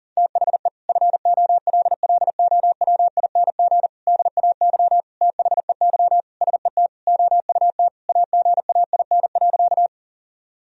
Morse code, heard as a keyed tone, about 37 wpm, 700 Hz. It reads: THE FOLLOWING DAY THEY SET OUT AGAIN.
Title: Sentences from Top 1000 Words - Review Part 4 - 40wpm